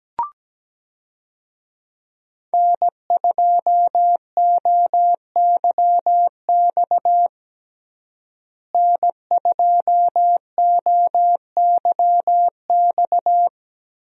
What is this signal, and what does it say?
Morse code, heard as a keyed tone, about 17 wpm, 700 Hz. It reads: N2OYX N2OYX
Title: USA Call Signs - 5 Characters - Repeated 2x - 17wpm